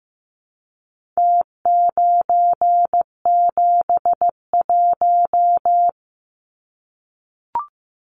Morse code, heard as a keyed tone, about 15 wpm, 700 Hz. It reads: T971
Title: Numbers - Contest Style - 4-Digit 15wpm